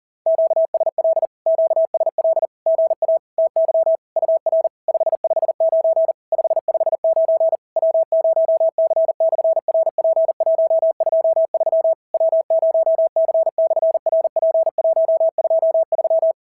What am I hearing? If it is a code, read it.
Morse code, heard as a keyed tone, about 38 wpm, 650 Hz. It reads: QSL QSL GA TY UR 559 559 W0C/RP123 W0C/RP123
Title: SOTA QSO Practice 40wpm